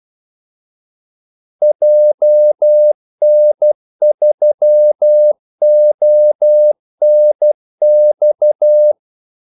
Morse code, heard as a keyed tone, about 12 wpm, 600 Hz. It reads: JN3ONX
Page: Kaldesignaler 12 wpm